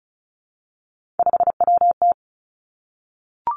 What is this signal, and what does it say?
Morse code, heard as a keyed tone, about 36 wpm, 700 Hz. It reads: 5WT
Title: Three Challenging Letter-Number Combinations 35wpm